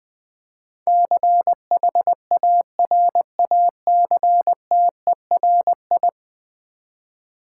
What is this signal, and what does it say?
Morse code, heard as a keyed tone, about 20 wpm, 700 Hz. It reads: CHARACTERI
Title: Build A Word - 5 Letters At A Time - 12-Letter Words - 20wpm